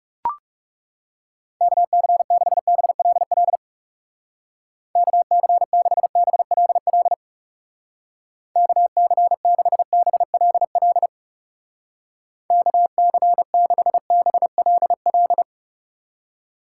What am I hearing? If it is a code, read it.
Morse code, heard as a keyed tone, about 38 wpm, 690 Hz. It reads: KC6BLL KC6BLL KC6BLL KC6BLL